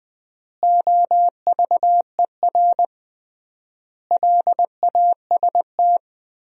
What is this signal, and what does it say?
Morse code, heard as a keyed tone, about 20 wpm, 700 Hz. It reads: OVER LAST
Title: 3x Word-Spacing - Sentences from Top 300 Words - 20wpm